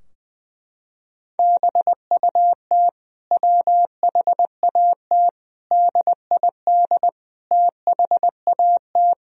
Morse code, heard as a keyed tone, about 20 wpm, 700 Hz. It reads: BUT WHAT DID THAT
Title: Sentences from Top 1000 Words - Review Part 4 - 20wpm